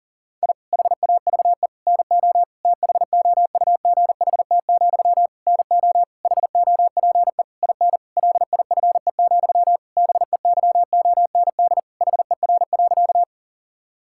Morse code, heard as a keyed tone, about 37 wpm, 700 Hz. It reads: I HAVE NO THOUGHT, NO HOPE IN LIFE, BEYOND HER.
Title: Sentences from Top 1000 Words - Review Part 6 - 40wpm